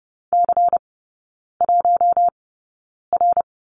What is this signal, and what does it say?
Morse code, heard as a keyed tone, about 30 wpm, spaced as for 9 wpm, 700 Hz. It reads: C1R
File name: Lesson 16e - Letter c - Three Characters 30wpm